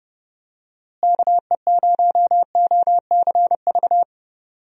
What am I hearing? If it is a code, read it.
Morse code, heard as a keyed tone, about 30 wpm, 700 Hz. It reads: KE0OCV
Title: USA Call Signs - 6 Characters - Repeated 2x - 30wpm